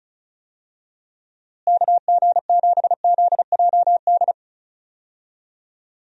Morse code, heard as a keyed tone, about 34 wpm, 700 Hz. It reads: KG7ZJD
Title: USA Call Signs - 6 Characters - Repeated 3x - 35wpm